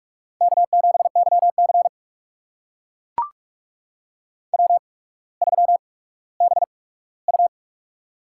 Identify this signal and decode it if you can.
Morse code, heard as a keyed tone, about 44 wpm, spaced as for 19 wpm, 700 Hz. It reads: K7YC W3BU
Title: USA Call Signs - 4 Characters - 8x Character Spacing - 45wpm